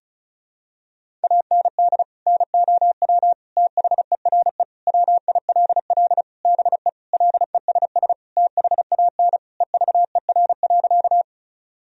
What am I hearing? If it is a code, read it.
Morse code, heard as a keyed tone, about 36 wpm, 700 Hz. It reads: AND NOW THERE WILL BE LESS THAN EVER.
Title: Sentences from Top 1000 Words - Review Part 2 - 35wpm